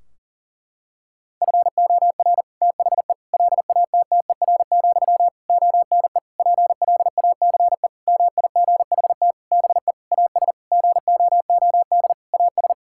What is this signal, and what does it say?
Morse code, heard as a keyed tone, about 40 wpm, 700 Hz. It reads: FOR THE LATTER, ONE PLACE MIGHT BE AS GOOD AS
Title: Sentences from Top 1600 Words 40wpm